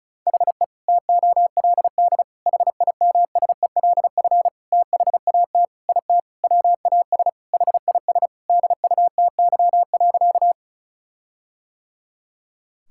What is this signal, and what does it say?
Morse code, heard as a keyed tone, about 34 wpm, 700 Hz. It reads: HE TOLD HIMSELF THAT IT WAS HIS DUTY.
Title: Sentences from Top 1600 Words 35wpm